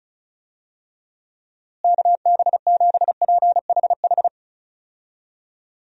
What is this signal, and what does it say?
Morse code, heard as a keyed tone, about 36 wpm, 700 Hz. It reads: KB7PHH
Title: USA Call Signs - 6 Characters - Repeated 3x - 35wpm